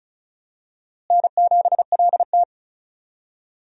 Morse code, heard as a keyed tone, about 36 wpm, 700 Hz. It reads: N7LT
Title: USA Call Signs - 4 Characters - Repeated 2x - 35wpm